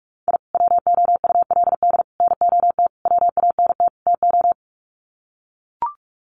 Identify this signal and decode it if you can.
Morse code, heard as a keyed tone, about 44 wpm, 700 Hz. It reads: I WOULD NOT WANT TO
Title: Sets of 5 Words 45wpm